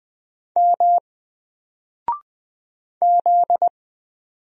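Morse code, heard as a keyed tone, about 20 wpm, 700 Hz. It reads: M Z
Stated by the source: Single letter and number Morse drill at 20 wpm, from N4LCD.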